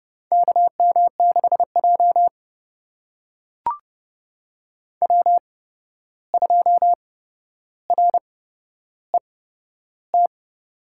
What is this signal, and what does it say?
Morse code, heard as a keyed tone, about 30 wpm, spaced as for 12 wpm, 700 Hz. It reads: KM6J W2RET
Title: USA Call Signs - 8x Character Spacing - 30wpm